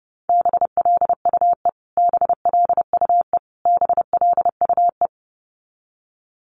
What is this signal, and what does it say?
Morse code, heard as a keyed tone, about 29 wpm, 700 Hz. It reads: BLUE BLUE BLUE
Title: Top 500 Words Repeated 30wpm